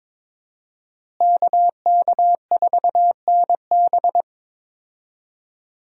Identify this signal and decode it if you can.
Morse code, heard as a keyed tone, about 22 wpm, 700 Hz. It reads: KK4NB
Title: USA Call Signs - 5 Characters - Repeated 3x - 22wpm